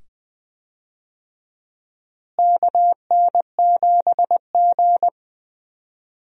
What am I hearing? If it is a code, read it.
Morse code, heard as a keyed tone, about 20 wpm, 700 Hz. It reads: KN7G
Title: USA Call Signs - 4 Characters - Repeated 2x - 20wpm